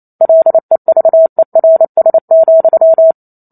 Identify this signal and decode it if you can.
Morse code, heard as a keyed tone, about 28 wpm, 650 Hz. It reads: LEVERS,